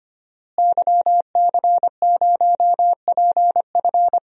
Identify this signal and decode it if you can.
Morse code, heard as a keyed tone, about 25 wpm, 700 Hz. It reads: YC0PF
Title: DXCC-100 Call Signs - 5 Characters - Repeated 2x - 25wpm